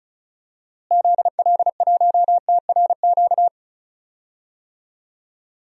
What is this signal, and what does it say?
Morse code, heard as a keyed tone, about 36 wpm, 700 Hz. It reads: ZL1TRQ
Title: DXCC-100 Call Signs - 6 Characters - Repeated 2x - 35wpm